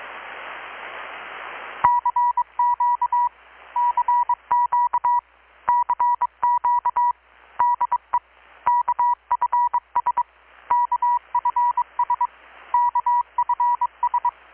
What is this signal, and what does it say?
Morse code, heard as a keyed tone, about 22 wpm, 970 Hz. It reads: CQ CQ CQ DE KFS KFS KFS
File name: blog / swl / 6-abril-2019-12695-USB-KPS